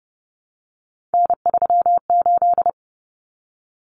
Morse code, heard as a keyed tone, about 30 wpm, 700 Hz. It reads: N38
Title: Common Cut Numbers - 3-Digit 30wpm